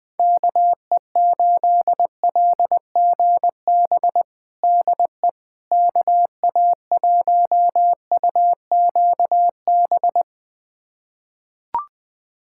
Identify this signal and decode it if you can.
Morse code, heard as a keyed tone, about 20 wpm, 700 Hz. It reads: KE8LGB DE KA1UQB
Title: USA Callsign DE Callsign - 6 Characters - 20wpm